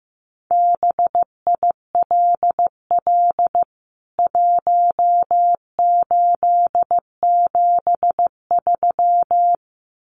Morse code, heard as a keyed tone, about 15 wpm, 700 Hz. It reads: BILL 1873
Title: CWT Exchange 15wpm